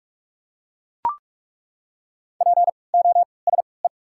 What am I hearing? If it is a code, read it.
Morse code, heard as a keyed tone, about 42 wpm, spaced as for 25 wpm, 690 Hz. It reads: POSE
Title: 3x Character Spacing - Top 500 Words - 45wpm